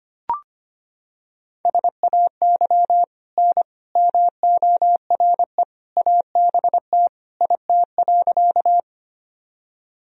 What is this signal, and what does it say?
Morse code, heard as a keyed tone, about 25 wpm, 700 Hz. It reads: SAY N MORE ABT IT.
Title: Sentences from Top 200 Words and Abbreviations 25wpm